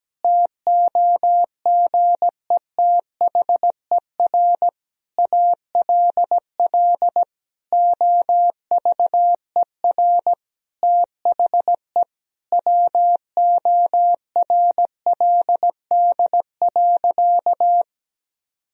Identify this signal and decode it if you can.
Morse code, heard as a keyed tone, about 17 wpm, 700 Hz. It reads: TOGETHER ALL OVER THE WORLD.
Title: Sentences from Top 1000 Words - Review Part 2 - 17wpm